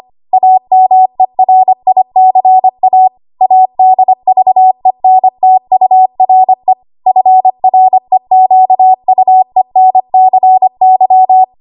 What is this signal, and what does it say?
Morse code, heard as a keyed tone, about 25 wpm, 750 Hz. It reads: AMERICA ADVENTURE FREQUENCY
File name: Morse / 25WPM